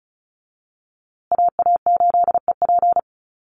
Morse code, heard as a keyed tone, about 34 wpm, 700 Hz. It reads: AA8EP
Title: USA Call Signs Repeated 3x 35wpm